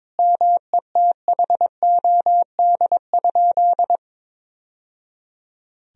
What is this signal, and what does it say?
Morse code, heard as a keyed tone, about 22 wpm, 700 Hz. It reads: METHOD?
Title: Sentences from Top 1000 Words - Review Part 3 - 22wpm